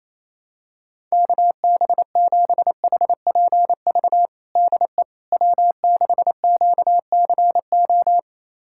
Morse code, heard as a keyed tone, about 28 wpm, 700 Hz. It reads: KB7HPV DE W6QCO
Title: USA Callsign DE Callsign 28wpm